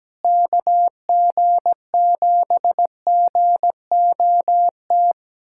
Morse code, heard as a keyed tone, about 17 wpm, 700 Hz. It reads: KG7GOT
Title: USA Callsign DE Callsign 17wpm